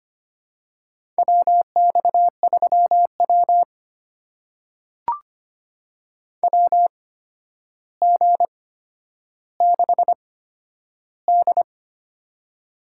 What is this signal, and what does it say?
Morse code, heard as a keyed tone, about 25 wpm, spaced as for 11 wpm, 700 Hz. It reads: WX3W WG6D
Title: USA Call Signs - 4 Characters - 8x Character Spacing - 25wpm